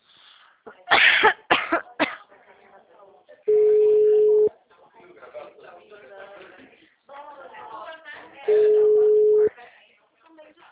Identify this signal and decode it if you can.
Ringback tone (Europe, 425 Hz), with speech.